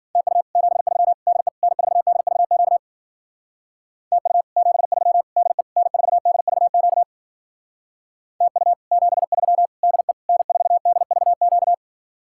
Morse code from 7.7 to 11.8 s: TU 73 DE N4DVQ

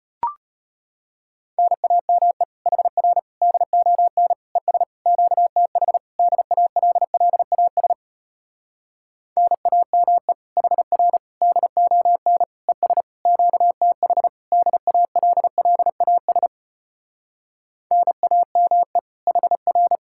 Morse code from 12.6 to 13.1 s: ES

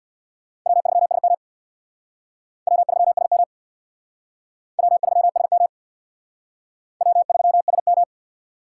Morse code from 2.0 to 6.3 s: W3SM W3SM

